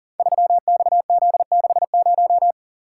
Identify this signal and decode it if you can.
Morse code, heard as a keyed tone, about 42 wpm, 690 Hz. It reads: 3XZ60
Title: Code Groups - 5 Letter:Numbers - 40wpm